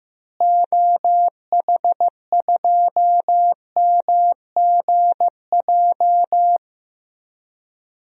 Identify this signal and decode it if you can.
Morse code, heard as a keyed tone, about 15 wpm, 700 Hz. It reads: OH2MGJ
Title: DXCC-100 Call Signs - 6 Characters - Repeated 3x - 15wpm